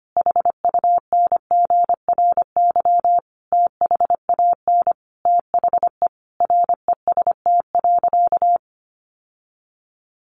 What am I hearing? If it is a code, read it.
Morse code, heard as a keyed tone, about 25 wpm, 700 Hz. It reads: HUNGRY THAN THE REST.
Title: Sentences from Top 1400 Words 25wpm